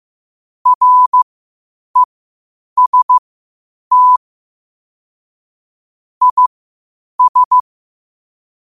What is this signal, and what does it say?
Morse code, heard as a keyed tone, about 15 wpm, spaced as for 8 wpm, 1000 Hz. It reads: REST IS
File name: qotdfiles / 8wpm